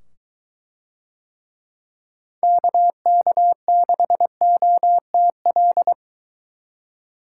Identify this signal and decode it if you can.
Morse code, heard as a keyed tone, about 22 wpm, 700 Hz. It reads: KK6OTL